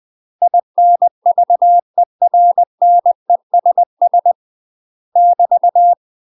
Morse code, heard as a keyed tone, about 20 wpm, 700 Hz. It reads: INVERNESS <BT>
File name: UK MASTER QSOs TRAINING / 20WPM / UK MASTER QSOs 0012